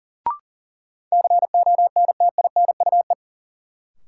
Morse code, heard as a keyed tone, about 40 wpm, 700 Hz. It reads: CONTINUE
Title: Sentences from Top 1000 Words - Review Part 6 - New Word Drill - 40wpm